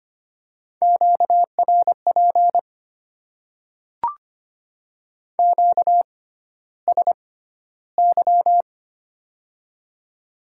Morse code, heard as a keyed tone, about 25 wpm, spaced as for 13 wpm, 700 Hz. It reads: QRP QSY
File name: Top 90 QSO Elements - 6x Character Spacing - 25wpm